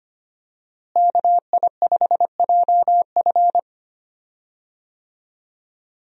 Morse code, heard as a keyed tone, about 25 wpm, 700 Hz. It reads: KI5JF
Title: USA Call Signs - 5 Characters - Repeated 3x - 25wpm